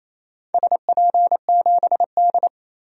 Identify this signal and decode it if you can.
Morse code, heard as a keyed tone, about 28 wpm, 700 Hz. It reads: SP7D
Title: DXCC-100 Call Signs - 4 Characters - 28wpm